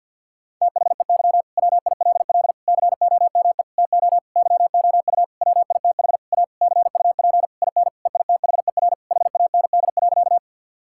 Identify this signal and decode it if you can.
Morse code, heard as a keyed tone, about 50 wpm, 700 Hz. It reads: THEY WILL COME TO YOU WITH A CUP IN EITHER HAND.
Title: Sentences from Top 1600 Words 50wpm